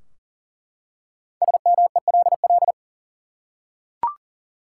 Morse code, heard as a keyed tone, about 40 wpm, 700 Hz. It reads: SMELL
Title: Five Letter Words - 40wpm